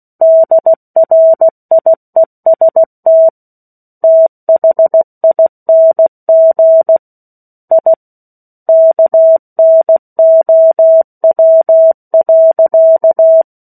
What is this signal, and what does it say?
Morse code, heard as a keyed tone, about 16 wpm, 650 Hz. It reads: DRIEST THING I KNOW.
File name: 03-Chapter-03 16 WPM